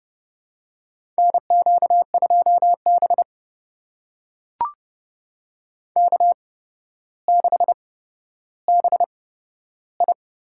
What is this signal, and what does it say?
Morse code, heard as a keyed tone, about 31 wpm, spaced as for 13 wpm, 700 Hz. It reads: NQ2B K6BI
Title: USA Call Signs - 4 Characters - 8x Character Spacing - 30wpm